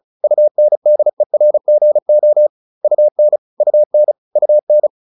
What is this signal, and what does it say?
Morse code, heard as a keyed tone, about 35 wpm, 600 Hz. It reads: UNDERGO UN UN UN